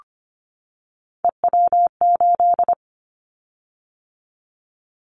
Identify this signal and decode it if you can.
Morse code, heard as a keyed tone, about 25 wpm, 700 Hz. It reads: EW8
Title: Callsign Prefixes for Top 50 DXCC - 25wpm